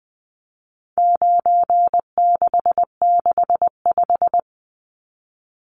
Morse code, heard as a keyed tone, about 20 wpm, 700 Hz. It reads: 9665
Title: Numbers - 4-Digit 20wpm